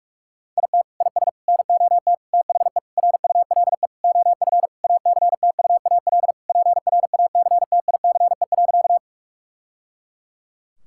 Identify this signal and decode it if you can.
Morse code, heard as a keyed tone, about 47 wpm, 700 Hz. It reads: IT IS NOT THE RULE OF ACTUAL PRACTICE.
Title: Sentences from Top 1000 Words - Review Part 5 - 45wpm